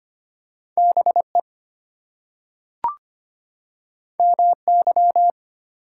Morse code, heard as a keyed tone, about 25 wpm, 700 Hz. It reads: BE MY
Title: Two Letter Words - 25wpm